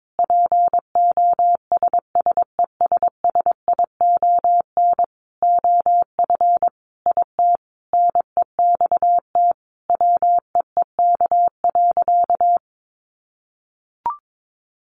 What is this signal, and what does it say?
Morse code, heard as a keyed tone, about 22 wpm, 700 Hz. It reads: POSSESSION OF IT NEXT WEEK.
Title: Sentences from Top 1400 Words 22wpm